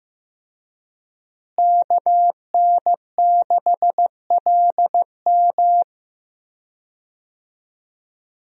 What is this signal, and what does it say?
Morse code, heard as a keyed tone, about 15 wpm, 700 Hz. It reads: KN6LM
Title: USA Call Signs - 5 Characters - Repeated 3x - 15wpm